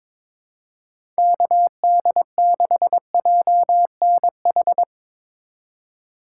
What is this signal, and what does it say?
Morse code, heard as a keyed tone, about 22 wpm, 700 Hz. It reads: KD6JNH
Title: USA Call Signs - 3x Character Spacing - 22wpm